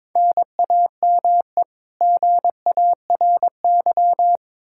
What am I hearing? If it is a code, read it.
Morse code, heard as a keyed tone, about 22 wpm, 700 Hz. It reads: NAME GARY
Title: Practice QSO - Exchange 1 - Beginning - 22wpm